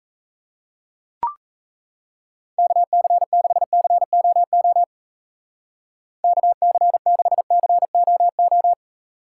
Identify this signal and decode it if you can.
Morse code, heard as a keyed tone, about 35 wpm, 700 Hz. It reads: KC6COO KC6COO
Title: Speed Racing - USA 2x3 Call Signs 28wpm